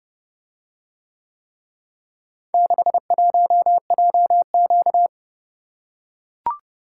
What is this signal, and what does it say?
Morse code, heard as a keyed tone, about 29 wpm, 700 Hz. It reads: 61JQ